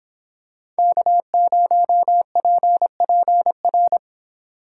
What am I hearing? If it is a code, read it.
Morse code, heard as a keyed tone, about 26 wpm, 700 Hz. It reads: K0PPR